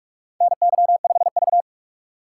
Morse code, heard as a keyed tone, about 46 wpm, 700 Hz. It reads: NY5V
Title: USA Call Signs - 4 Characters - 6x Character Spacing - 45wpm